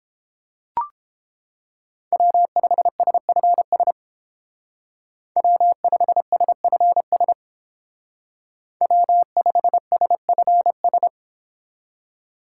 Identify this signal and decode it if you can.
Morse code, heard as a keyed tone, about 30 wpm, 700 Hz. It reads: W5SFS W5SFS W5SFS